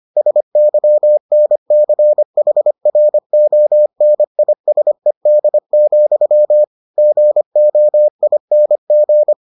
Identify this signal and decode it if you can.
Morse code, heard as a keyed tone, about 25 wpm, 600 Hz. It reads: SYNCHRONISED, GOING